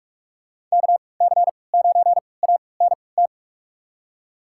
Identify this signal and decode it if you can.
Morse code, heard as a keyed tone, about 42 wpm, spaced as for 25 wpm, 700 Hz. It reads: KC9ANT
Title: USA Call Signs - 6 Characters - 3x Character Spacing - 45wpm